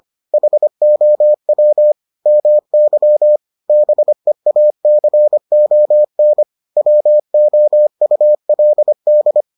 Morse code, heard as a keyed tone, about 25 wpm, 600 Hz. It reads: HOW MY BEACON WOULD